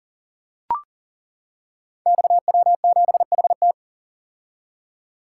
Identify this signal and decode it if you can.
Morse code, heard as a keyed tone, about 39 wpm, 700 Hz. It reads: XW7HT
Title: Code Groups - 5 Letter:Numbers - 40wpm